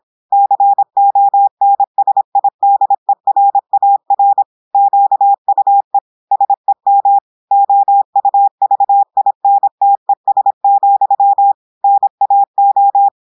Morse code, heard as a keyed tone, about 26 wpm, 800 Hz. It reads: CONSIDERAR QUE SEM OUVINTES, NAO
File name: portugais texte en clair 2 vitesse 26 mots minutes